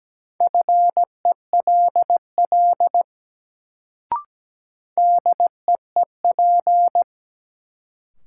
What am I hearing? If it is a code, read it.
Morse code, heard as a keyed tone, about 17 wpm, 700 Hz. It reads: FELL DEEP
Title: Sets of 5 Words Encore - New Word Drill - 17wpm